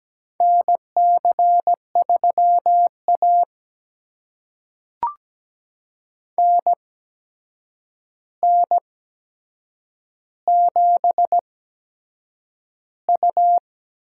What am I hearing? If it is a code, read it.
Morse code, heard as a keyed tone, about 17 wpm, spaced as for 7 wpm, 700 Hz. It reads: NC3A NN7U
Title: USA Call Signs - 4 Characters - 8x Character Spacing - 17wpm